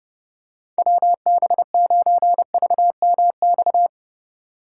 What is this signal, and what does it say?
Morse code, heard as a keyed tone, about 29 wpm, 700 Hz. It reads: WB9VMX